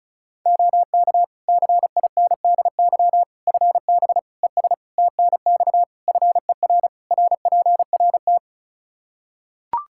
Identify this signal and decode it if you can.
Morse code, heard as a keyed tone, about 36 wpm, 700 Hz. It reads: OK CINDY FB ES TNX FER RPRT
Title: K1USN SST QSO Practice 35wpm